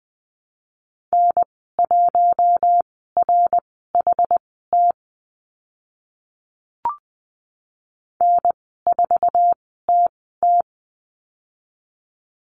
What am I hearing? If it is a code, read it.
Morse code, heard as a keyed tone, about 20 wpm, spaced as for 15 wpm, 700 Hz. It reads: N1RHT N4TT
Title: 082 Lesson 14g - Number 2 - Call Signs 20wpm